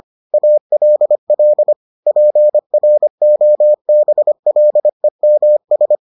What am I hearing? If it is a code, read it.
Morse code, heard as a keyed tone, about 25 wpm, 600 Hz. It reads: ALL PROBLEMS